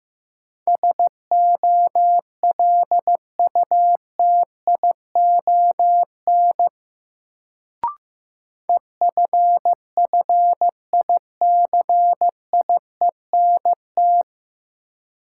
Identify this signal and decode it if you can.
Morse code, heard as a keyed tone, about 15 wpm, 700 Hz. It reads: SOLUTION EFFICIENT
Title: Sentences from Top 1300 Words - New Word Drill - 15wpm